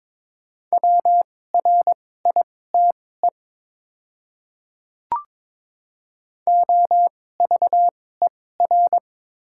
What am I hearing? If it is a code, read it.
Morse code, heard as a keyed tone, about 22 wpm, spaced as for 16 wpm, 700 Hz. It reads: WRITE OVER
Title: 2x Character Spacing - Top 300 Words - 22wpm